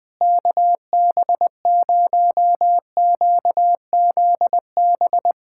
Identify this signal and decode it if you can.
Morse code, heard as a keyed tone, about 20 wpm, 700 Hz. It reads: KB0QZB
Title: USA Callsign DE Callsign 20wpm